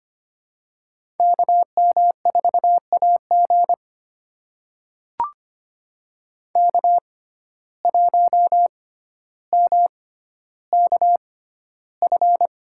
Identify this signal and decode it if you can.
Morse code, heard as a keyed tone, about 25 wpm, spaced as for 13 wpm, 700 Hz. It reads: KM4AG K1MKF
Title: USA Call Signs - 5 Characters - 6x Character Spacing - 25wpm